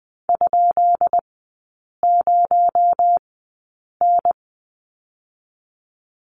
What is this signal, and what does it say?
Morse code, heard as a keyed tone, about 20 wpm, spaced as for 8 wpm, 700 Hz. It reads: ?0N
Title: 218 Lesson 35e - Number 0 - Three Characters 20wpm